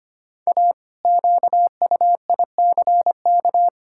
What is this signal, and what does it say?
Morse code, heard as a keyed tone, about 25 wpm, 700 Hz. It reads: A QUICK
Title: Pangram Challenge v2 - 25wpm